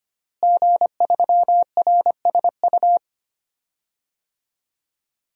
Morse code, heard as a keyed tone, about 25 wpm, 700 Hz. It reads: G3RSU
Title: DXCC-100 Call Signs - 5 Characters - Repeated 2x - 25wpm